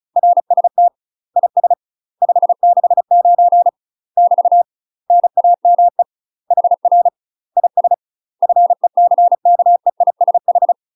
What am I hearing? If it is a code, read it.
Morse code, heard as a keyed tone, about 36 wpm, 700 Hz. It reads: RST IS 569 <BT> NAME HR IS FECKEISH
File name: UK MASTER QSOs TRAINING / 35WPM / UK MASTER QSOs 0033